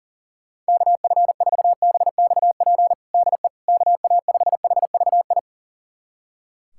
Morse code, heard as a keyed tone, about 40 wpm, 700 Hz. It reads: KF4BXP DE KA5HVI
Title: USA Callsign DE Callsign - 6 Characters - 40wpm